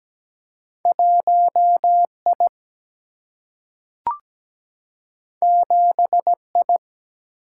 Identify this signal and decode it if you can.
Morse code, heard as a keyed tone, about 17 wpm, 700 Hz. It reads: 1 I 7 I
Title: Single-Letter Single-Number 17wpm